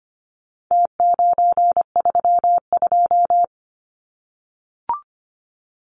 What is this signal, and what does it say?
Morse code, heard as a keyed tone, about 25 wpm, 700 Hz. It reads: T932
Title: Numbers - Contest Style - 4-Digit 25wpm